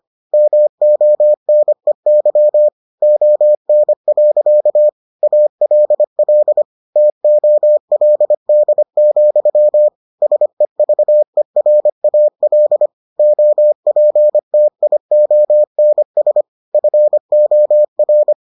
Morse code, heard as a keyed tone, about 25 wpm, 600 Hz. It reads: MONEY ON. ALL TOLD, SEVERAL OPTIONS FOR